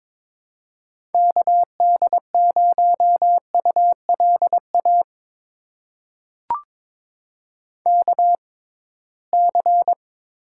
Morse code, heard as a keyed tone, about 21 wpm, spaced as for 16 wpm, 700 Hz. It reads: KD0ULA KC
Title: USA Call Signs - 6 Characters - 6x Character Spacing - 22wpm